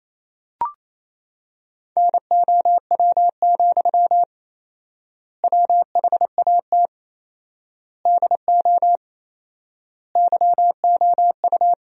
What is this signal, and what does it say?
Morse code, heard as a keyed tone, about 28 wpm, 700 Hz. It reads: NOW, WHAT DO YOU
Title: 4x Word-Spacing - Sentences from Top 200 Words - 28wpm